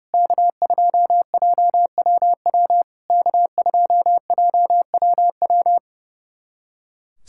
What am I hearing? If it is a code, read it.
Morse code, heard as a keyed tone, about 30 wpm, 700 Hz. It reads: K2JWW K2JWW
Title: SOTA QSO Practice 30wpm